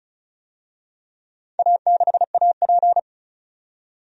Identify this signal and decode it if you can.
Morse code, heard as a keyed tone, about 34 wpm, 700 Hz. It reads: A6AP
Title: DXCC-100 Call Signs - 4 Characters - 35wpm